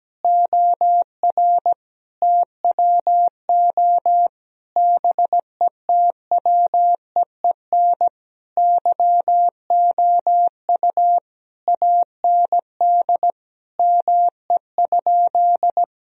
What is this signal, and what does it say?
Morse code, heard as a keyed tone, about 17 wpm, 700 Hz. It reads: OR TWO BETWEEN YOU AND ME?
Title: Sentences from Top 1000 Words - Review Part 2 - 17wpm